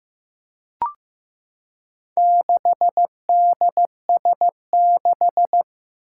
Morse code, heard as a keyed tone, about 15 wpm, 700 Hz. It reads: 6DS6
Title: Four Challenging Letter-Number Combinations 15wpm